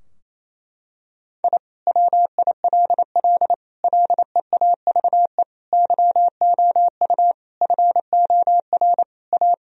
Morse code, heard as a keyed tone, about 28 wpm, 700 Hz. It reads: I WILL LEAVE YOU FOR A
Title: Sentences from Top 1000 Words - Review Part 4 - 28wpm